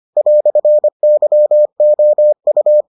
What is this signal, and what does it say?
Morse code, heard as a keyed tone, about 25 wpm, 600 Hz. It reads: "YOU